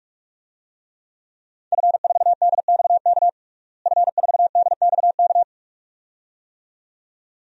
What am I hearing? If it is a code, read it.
Morse code, heard as a keyed tone, about 46 wpm, 700 Hz. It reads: F4DXK F4DXK
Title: DXCC-100 Call Signs - 5 Characters - Repeated 2x - 45wpm